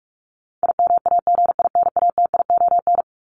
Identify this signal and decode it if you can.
Morse code, heard as a keyed tone, about 44 wpm, 700 Hz. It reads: IMAGINATION
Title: Top 1001-1250 Words 45wpm